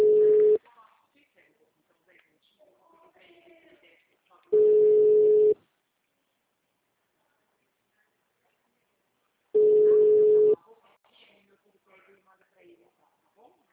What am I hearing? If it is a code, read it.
Ringback tone (Europe, 425 Hz), with speech.